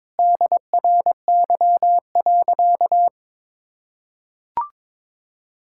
Morse code, heard as a keyed tone, about 21 wpm, 700 Hz. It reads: DRY.
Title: Sentences from Top 1600 Words 22wpm